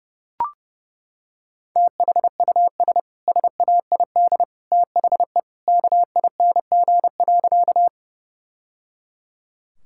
Morse code, heard as a keyed tone, about 30 wpm, 700 Hz. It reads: THUS SAID THE KING.
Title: Sentences from Top 1000 Words - Review Part 5 - 30wpm